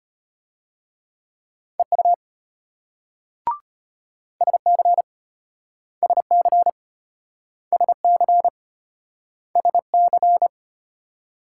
Morse code, heard as a keyed tone, about 34 wpm, 700 Hz. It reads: EU SC SC SC SC